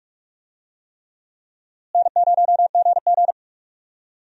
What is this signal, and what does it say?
Morse code, heard as a keyed tone, about 45 wpm, 700 Hz. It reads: N0GG
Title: USA Call Signs - 4 Characters - Repeated 3x - 45wpm